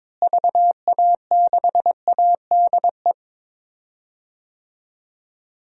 Morse code, heard as a keyed tone, about 22 wpm, 700 Hz. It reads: VA6ADE
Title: DXCC-100 Call Signs - 6 Characters - Repeated 2x - 22wpm